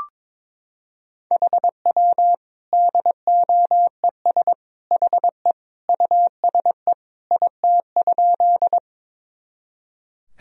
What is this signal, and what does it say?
Morse code, heard as a keyed tone, about 22 wpm, 700 Hz. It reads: HW DOES HE USE IT?
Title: Sentences from Top 200 Words and Abbreviations 22wpm